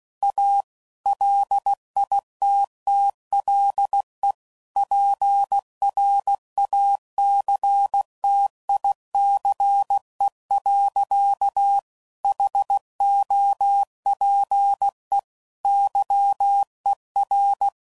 Morse code, heard as a keyed tone, about 16 wpm, 800 Hz. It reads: A LITTLE PRACTICE. HOPE YER
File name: CW Code Groups @15WPM / QSOs / WD0FBT